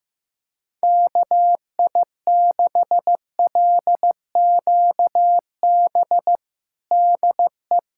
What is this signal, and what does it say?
Morse code, heard as a keyed tone, about 15 wpm, 700 Hz. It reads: KI6LQB DE
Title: USA Callsign DE Callsign - 6 Characters - 15wpm